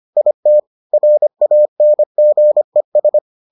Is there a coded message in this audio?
Yes: Morse code from 0.2 to 3.2 s, heard as a keyed tone, about 25 wpm, 600 Hz. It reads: IT RANGES